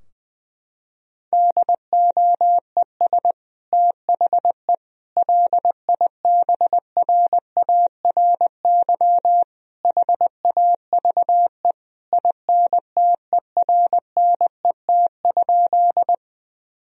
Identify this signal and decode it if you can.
Morse code, heard as a keyed tone, about 21 wpm, 700 Hz. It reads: DOES THE LIBRARY HAVE INTERNET?
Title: Sentences from Top 1000 Words - Review Part 4 - 20wpm